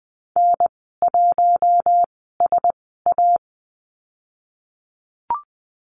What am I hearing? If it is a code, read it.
Morse code, heard as a keyed tone, about 20 wpm, spaced as for 15 wpm, 700 Hz. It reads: N1SA
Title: 055 Lesson 10g - Letter r - Call Signs 20wpm